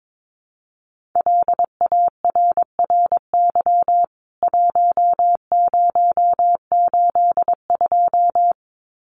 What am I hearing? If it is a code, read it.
Morse code, heard as a keyed tone, about 22 wpm, 700 Hz. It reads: LARRY 1082
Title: CWT Exchange 22wpm